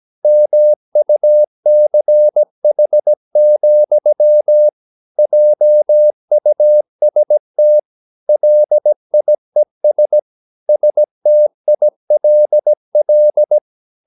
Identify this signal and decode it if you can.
Morse code, heard as a keyed tone, about 17 wpm, 600 Hz. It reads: MUCH, JUST LIES STILL
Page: Romanen Never af Ken Follet på engelsk.